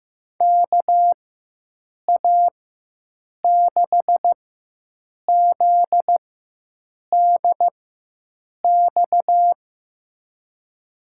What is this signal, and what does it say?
Morse code, heard as a keyed tone, about 15 wpm, spaced as for 7 wpm, 700 Hz. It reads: KA6ZDX
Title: USA Call Signs - 6 Characters - 4x Character Spacing - 15wpm